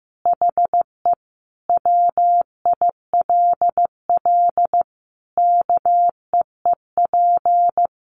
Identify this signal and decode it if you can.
Morse code, heard as a keyed tone, about 15 wpm, 700 Hz. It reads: HE WILL KEEP
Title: Sentences from Top 300 Words 15wpm